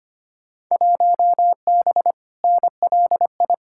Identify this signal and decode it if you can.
Morse code, heard as a keyed tone, about 25 wpm, 700 Hz. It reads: 1B NLI
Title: Field Day Exchanges - 25wpm